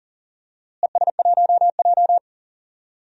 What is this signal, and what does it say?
Morse code, heard as a keyed tone, about 40 wpm, 700 Hz. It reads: ES1J
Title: DXCC-100 Call Signs - 4 Characters - Repeated 3x - 40wpm